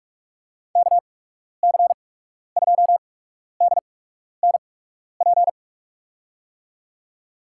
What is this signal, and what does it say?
Morse code, heard as a keyed tone, about 47 wpm, spaced as for 12 wpm, 700 Hz. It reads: KC2DNP
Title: USA Call Signs - 6 Characters - 8x Character Spacing - 45wpm